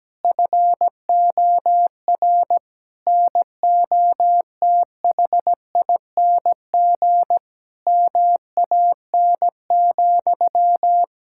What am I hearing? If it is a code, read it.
Morse code, heard as a keyed tone, about 17 wpm, 700 Hz. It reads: FOR NOTHING MAN,
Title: Sentences from Top 1000 Words - Review Part 6 - 17wpm